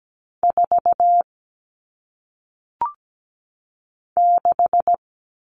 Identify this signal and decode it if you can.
Morse code, heard as a keyed tone, about 17 wpm, 700 Hz. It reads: 4 6
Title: All Cut Numbers - 1-Digit 17wpm